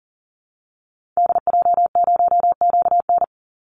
Morse code, heard as a keyed tone, about 40 wpm, 700 Hz. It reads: DJ0QN
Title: Call Signs Repeated 2x - 40wpm